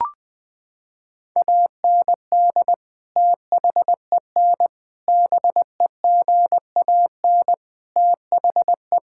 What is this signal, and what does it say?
Morse code, heard as a keyed tone, about 20 wpm, 700 Hz. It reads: AND THEN BEGAN THE
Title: Sentences from Top 1000 Words - Review Part 5 - 20wpm